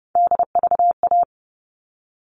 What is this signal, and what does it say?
Morse code, heard as a keyed tone, about 29 wpm, 700 Hz. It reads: DVA
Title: All Cut Numbers - 3-Digit 30wpm